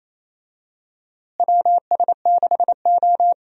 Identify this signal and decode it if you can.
Morse code, heard as a keyed tone, about 27 wpm, 700 Hz. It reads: WS6O
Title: USA Call Signs - 4 Characters - Repeated 3x - 28wpm